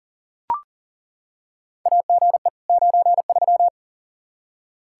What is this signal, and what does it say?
Morse code, heard as a keyed tone, about 40 wpm, 700 Hz. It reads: AGE 93
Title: K1USN SST QSO Practice 40wpm